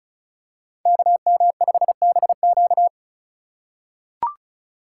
Morse code, heard as a keyed tone, about 36 wpm, 700 Hz. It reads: KM5BQ
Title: USA Call Signs - 5 Characters - 2x Character Spacing - 35wpm